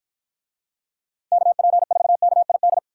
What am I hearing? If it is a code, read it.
Morse code, heard as a keyed tone, about 52 wpm, 700 Hz. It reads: KP4KID